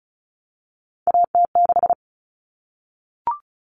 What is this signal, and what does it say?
Morse code, heard as a keyed tone, about 34 wpm, 700 Hz. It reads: AT6
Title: Common Cut Numbers - 3-Digit 35wpm